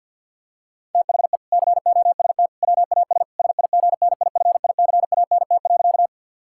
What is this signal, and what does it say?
Morse code, heard as a keyed tone, about 50 wpm, 700 Hz. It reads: THE COST WAS SIGNIFICANT.
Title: Sentences from Top 1000 Words - Review Part 6 - 50wpm